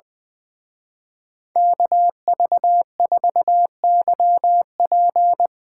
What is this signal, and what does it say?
Morse code, heard as a keyed tone, about 20 wpm, 700 Hz. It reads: KV4YP
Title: Speed Racing - USA Call Signs 20wpm